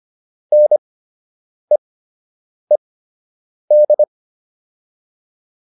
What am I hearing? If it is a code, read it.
Morse code, heard as a keyed tone, about 25 wpm, spaced as for 8 wpm, 600 Hz. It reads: NEED